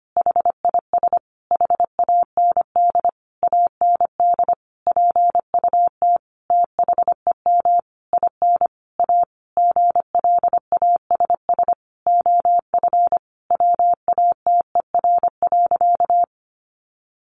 Morse code, heard as a keyed tone, about 25 wpm, 700 Hz. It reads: HIS HAND AND PUT THEM IN A GLASS OF WATER.
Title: Sentences from Top 1300 Words 25wpm